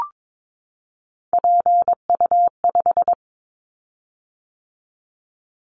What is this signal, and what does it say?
Morse code, heard as a keyed tone, about 22 wpm, 700 Hz. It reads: PU5
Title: Callsign Prefixes for Top 50 DXCC - 22wpm